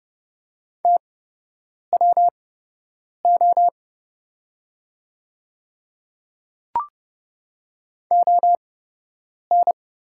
Morse code, heard as a keyed tone, about 31 wpm, spaced as for 8 wpm, 700 Hz. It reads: TWO ON
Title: 8x Character Spacing - Top 300 Words - 30wpm